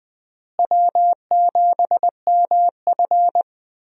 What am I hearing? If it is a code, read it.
Morse code, heard as a keyed tone, about 21 wpm, 700 Hz. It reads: W7MF
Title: USA Call Signs - 4 Characters - Repeated 2x - 20wpm